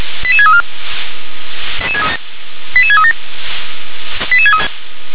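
Five-tone selective call (ZVEI), heard as quick runs of five tones, tones of about 70 ms. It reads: 88634 88637
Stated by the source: ZVEI five-tone selective call 88631 received by SDR.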